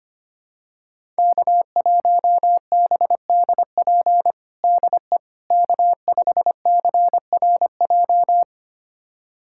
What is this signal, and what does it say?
Morse code, heard as a keyed tone, about 25 wpm, 700 Hz. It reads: K1BDP DE K5CRJ
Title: USA Callsign DE Callsign - 5 Characters - 25wpm